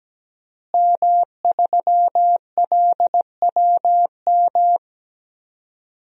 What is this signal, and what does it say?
Morse code, heard as a keyed tone, about 17 wpm, 700 Hz. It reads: M3LWM
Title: DXCC-100 Call Signs - 5 Characters - Repeated 2x - 17wpm